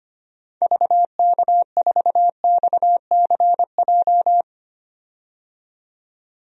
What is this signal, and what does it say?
Morse code, heard as a keyed tone, about 25 wpm, 700 Hz. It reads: VK4XCJ
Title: DXCC-100 Call Signs - 6 Characters - 25wpm